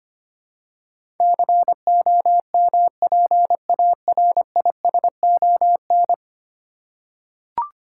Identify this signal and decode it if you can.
Morse code, heard as a keyed tone, about 25 wpm, 700 Hz. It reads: COMPARISON
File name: 10 Letter Words - First 3 Letters Given - 25wpm